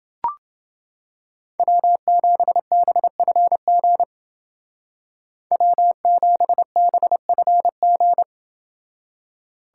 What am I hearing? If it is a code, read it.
Morse code, heard as a keyed tone, about 29 wpm, 700 Hz. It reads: W7BFG W7BFG